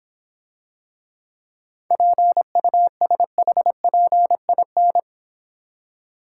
Morse code, heard as a keyed tone, about 26 wpm, 700 Hz. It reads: PUSHPIN